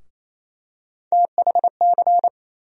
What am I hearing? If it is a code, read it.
Morse code, heard as a keyed tone, about 28 wpm, 700 Hz. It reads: THC
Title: USA Call Sign Postfixes - 3 Letters 28wpm